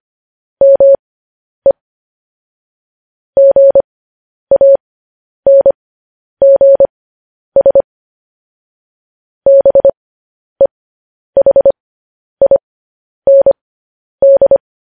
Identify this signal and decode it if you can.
Morse code, heard as a keyed tone, about 25 wpm, spaced as for 10 wpm, 550 Hz. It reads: ME GANGS BEHIND
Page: news headlines in morse code at 10 wpm